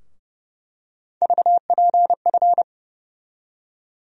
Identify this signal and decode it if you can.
Morse code, heard as a keyed tone, about 30 wpm, 700 Hz. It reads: VPF
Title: USA Call Sign Postfixes - 3 Letters 30wpm